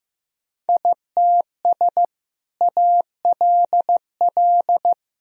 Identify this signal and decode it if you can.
Morse code, heard as a keyed tone, about 16 wpm, 700 Hz. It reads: ITS ALL
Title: Sentences from Top 1000 Words - Review Part 4 - 15wpm